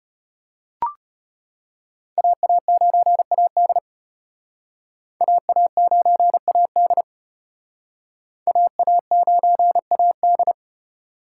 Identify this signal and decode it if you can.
Morse code, heard as a keyed tone, about 33 wpm, 700 Hz. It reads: AA9AD AA9AD AA9AD